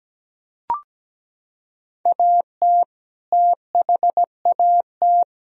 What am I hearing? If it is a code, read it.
Morse code, heard as a keyed tone, about 17 wpm, 700 Hz. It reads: AT THAT
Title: Sentences from Top 1000 Words - Review Part 1 - 17wpm